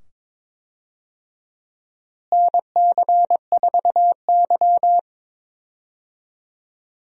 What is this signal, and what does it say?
Morse code, heard as a keyed tone, about 22 wpm, 700 Hz. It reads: NC4Y
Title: USA Call Signs - 4 Characters - Repeated 2x - 22wpm